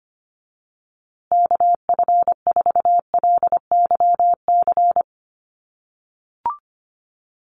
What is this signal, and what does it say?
Morse code, heard as a keyed tone, about 25 wpm, 700 Hz. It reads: KF4LYC
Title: USA Call Signs Repeated 2x 25wpm